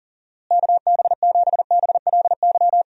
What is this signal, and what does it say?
Morse code, heard as a keyed tone, about 40 wpm, 700 Hz. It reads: KB7BLY
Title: USA Call Signs - 6 Characters - Repeated 3x - 40wpm